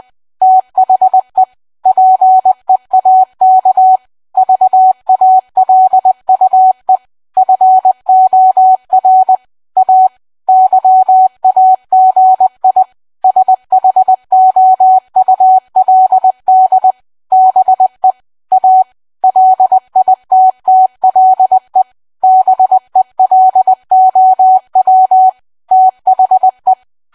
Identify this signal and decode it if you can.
Morse code, heard as a keyed tone, about 20 wpm, 750 Hz. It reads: THE PEAK VALUE FOR A YAGI SHOULD BE A LITTLE BELOW THE